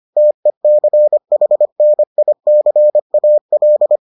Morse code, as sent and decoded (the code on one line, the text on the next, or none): - . -.-. .... -. .. -.-. .- .-..
TECHNICAL